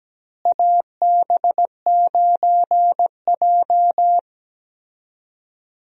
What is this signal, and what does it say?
Morse code, heard as a keyed tone, about 17 wpm, 700 Hz. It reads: AB9J
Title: USA Call Signs - 4 Characters - Repeated 2x - 17wpm